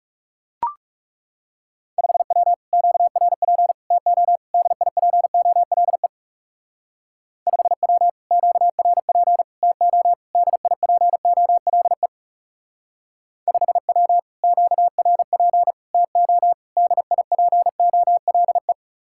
Morse code, heard as a keyed tone, about 38 wpm, 700 Hz. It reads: 5W QRP TO DIPOLE 5W QRP TO DIPOLE 5W QRP TO DIPOLE